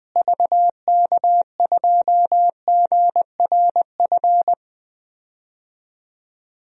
Morse code, heard as a keyed tone, about 20 wpm, 700 Hz. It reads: VK2GRF
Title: DXCC-100 Call Signs - 6 Characters - Repeated 3x - 20wpm